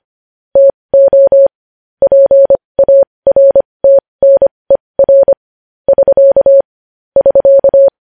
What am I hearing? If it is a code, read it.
Morse code, heard as a keyed tone, about 25 wpm, spaced as for 20 wpm, 550 Hz. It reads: TO PARTNER <SK> <SK>
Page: news-headlines-in-morse-code-at-20-wpm